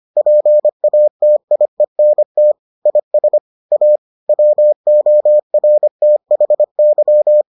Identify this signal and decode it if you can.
Morse code, heard as a keyed tone, about 25 wpm, 600 Hz. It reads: PATIENT IS A WORTHY